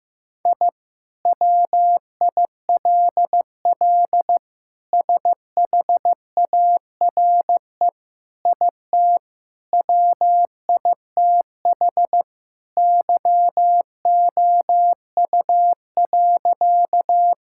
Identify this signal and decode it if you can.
Morse code, heard as a keyed tone, about 15 wpm, 700 Hz. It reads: I WILL SHARE IT WITH YOU.
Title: Sentences from Top 1000 Words - Review Part 6 - 15wpm